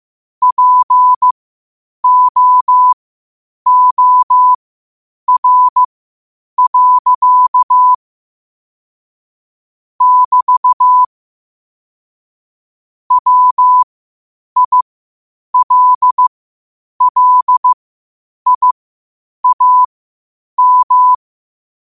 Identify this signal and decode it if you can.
Morse code, heard as a keyed tone, about 15 wpm, spaced as for 8 wpm, 1000 Hz. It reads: POOR. <BT> WILLIAM
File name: qotdfiles / 8wpm